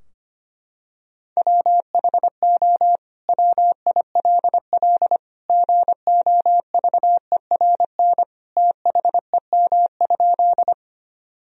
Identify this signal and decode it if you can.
Morse code, heard as a keyed tone, about 25 wpm, 700 Hz. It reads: WHO WILL GOVERN THEM?
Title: Sentences from Top 1000 Words - Review Part 2 - 25wpm